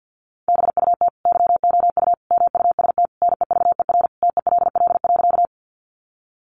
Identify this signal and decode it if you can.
Morse code, heard as a keyed tone, about 50 wpm, 700 Hz. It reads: BUT YOU MUST NEVER TELL.
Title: Sentences from Top 300 Words 50wpm